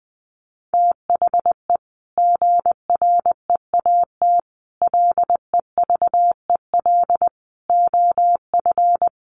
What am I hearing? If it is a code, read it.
Morse code, heard as a keyed tone, about 20 wpm, 700 Hz. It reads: THE GREAT LEVEL OF
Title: A Sound Of Thunder 20wpm